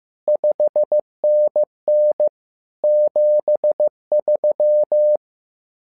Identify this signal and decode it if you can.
Morse code, heard as a keyed tone, about 15 wpm, 600 Hz. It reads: 5NN 73
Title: SOTA QSO Practice 15wpm